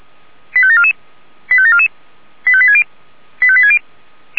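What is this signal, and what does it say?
Five-tone selective call (ZVEI), heard as quick runs of five tones, tones of about 70 ms. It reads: 85744 85744 85860 85860
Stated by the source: ZVEI five-tone selective call received on an SDR.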